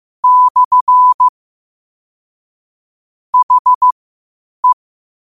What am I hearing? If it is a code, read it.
Morse code, heard as a keyed tone, about 15 wpm, spaced as for 8 wpm, 1000 Hz. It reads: / HE
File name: qotdfiles / 8wpm